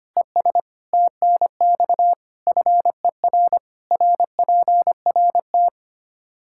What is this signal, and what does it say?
Morse code, heard as a keyed tone, about 25 wpm, 700 Hz. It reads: ES TNX FER RPRT
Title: Practice QSO - Exchange 2 - Info - 25wpm